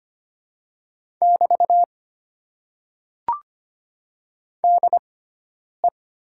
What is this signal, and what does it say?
Morse code, heard as a keyed tone, about 25 wpm, spaced as for 9 wpm, 700 Hz. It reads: <BT> DE
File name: Top 90 QSO Elements - 6x Character Spacing - 25wpm